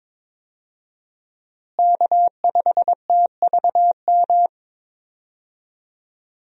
Morse code, heard as a keyed tone, about 22 wpm, 700 Hz. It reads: K5TVM
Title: USA Call Signs - 5 Characters - Repeated 2x - 22wpm